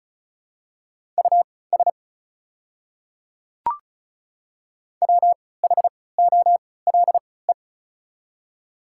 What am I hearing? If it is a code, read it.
Morse code, heard as a keyed tone, about 36 wpm, spaced as for 20 wpm, 700 Hz. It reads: US WHOLE
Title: 3x Character Spacing - Top 500 Words - 35wpm